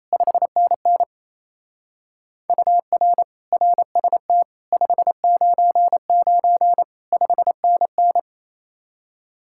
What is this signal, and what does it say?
Morse code, heard as a keyed tone, about 34 wpm, 700 Hz. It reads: 5NN UR RST 599 5NN
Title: Speed-Racing Common QSO Phrases Encore - 28wpm